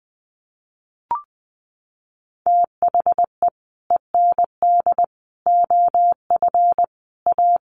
Morse code, heard as a keyed tone, about 20 wpm, 700 Hz. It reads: THE END OF A
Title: Sets of 5 Words Encore2 20wpm